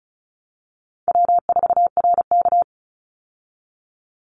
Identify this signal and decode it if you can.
Morse code, heard as a keyed tone, about 34 wpm, 700 Hz. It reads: W4RK
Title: Easy Call Signs Repeated 2x - 35wpm